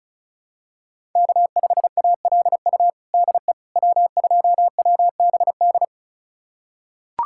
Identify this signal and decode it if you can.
Morse code, heard as a keyed tone, about 34 wpm, 700 Hz. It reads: K5ALU DE W2WBD
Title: USA Callsign DE Callsign - 5 Characters - 35wpm